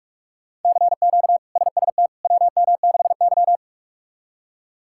Morse code, heard as a keyed tone, about 46 wpm, 700 Hz. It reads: CQ SST WM6Y
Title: K1USN SST QSO Practice - 45wpm